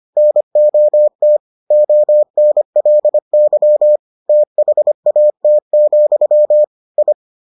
Morse code, heard as a keyed tone, about 25 wpm, 600 Hz. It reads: NOT ONLY THAT, I